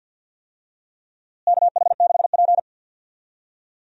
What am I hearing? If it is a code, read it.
Morse code, heard as a keyed tone, about 47 wpm, 700 Hz. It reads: KH6P
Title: DXCC-100 Call Signs - 4 Characters - Repeated 3x - 50wpm